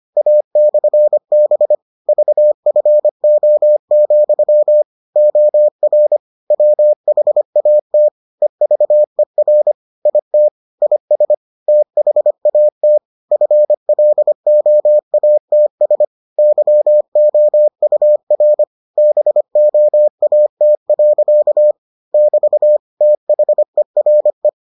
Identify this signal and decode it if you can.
Morse code, heard as a keyed tone, about 25 wpm, 600 Hz. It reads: A/B VFO, OR WHAT EVER IT IS THAT FLOATS YOUR BOAT. <BT> THERE